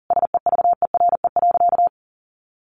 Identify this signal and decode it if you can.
Morse code, heard as a keyed tone, about 40 wpm, 700 Hz. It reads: SEVERE.
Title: Sentences from Top 1400 Words 40wpm